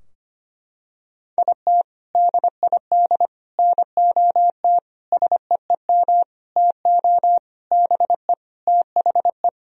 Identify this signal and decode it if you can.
Morse code, heard as a keyed tone, about 25 wpm, 700 Hz. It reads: IT DID NOT SEEM TO BE THE
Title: Sentences from Top 1000 Words - Review Part 2 - 25wpm